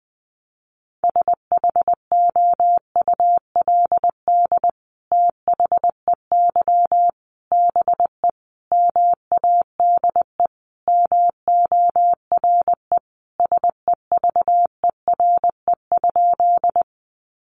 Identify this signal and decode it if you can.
Morse code, heard as a keyed tone, about 20 wpm, 700 Hz. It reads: SHOULD THEY BE MADE MORE SEVERE?
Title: Sentences from Top 1400 Words 20wpm